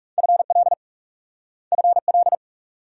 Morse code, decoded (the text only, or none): FL FL